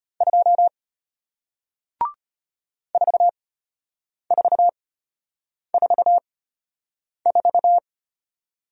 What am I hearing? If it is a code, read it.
Morse code, heard as a keyed tone, about 34 wpm, spaced as for 8 wpm, 700 Hz. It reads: 2 4444